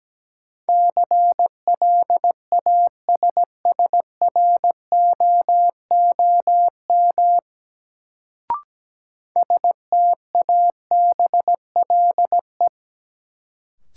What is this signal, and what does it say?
Morse code, heard as a keyed tone, about 17 wpm, 700 Hz. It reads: CLASSROOM STABLE
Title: Sentences from Top 1500 Words - New Word Drill - 17wpm